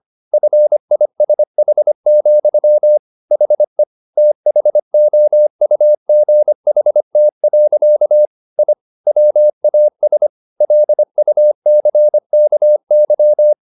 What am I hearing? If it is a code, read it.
Morse code, heard as a keyed tone, about 25 wpm, 600 Hz. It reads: FISH, HE THOUGHT. I WAS LUCKY
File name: the old man and the sea part 1 25wpm